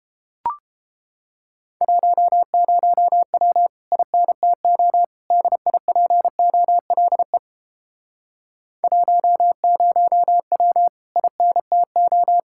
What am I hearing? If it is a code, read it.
Morse code, heard as a keyed tone, about 31 wpm, 700 Hz. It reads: 10W INTO DIPOLE 10W INTO